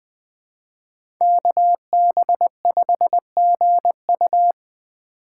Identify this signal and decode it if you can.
Morse code, heard as a keyed tone, about 20 wpm, 700 Hz. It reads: KB5GU
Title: USA Call Signs - 5 Characters - 3x Character Spacing - 20wpm